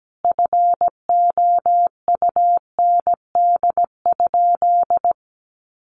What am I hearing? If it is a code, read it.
Morse code, heard as a keyed tone, about 17 wpm, 700 Hz. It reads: FOUND?
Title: Sentences from Top 300 Words 17wpm